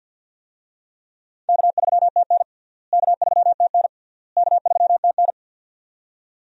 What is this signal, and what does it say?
Morse code, heard as a keyed tone, about 52 wpm, 700 Hz. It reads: K3TN K3TN K3TN
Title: USA Call Signs - 4 Characters - Repeated 3x - 50wpm